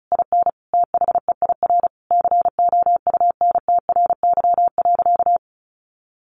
Morse code, heard as a keyed tone, about 34 wpm, 700 Hz. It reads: IN THEIR COUNTRY.
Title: Sentences from Top 1100 Words 35wpm